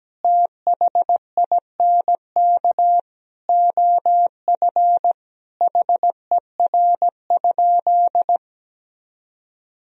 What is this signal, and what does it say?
Morse code, heard as a keyed tone, about 17 wpm, 700 Hz. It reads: THINK OF HER?
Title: Sentences from Top 200 Words and Abbreviations 17wpm